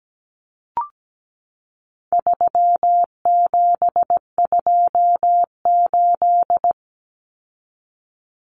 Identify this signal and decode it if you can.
Morse code, heard as a keyed tone, about 17 wpm, 700 Hz. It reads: 3728
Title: Common Cut Numbers - 4-Digit 17wpm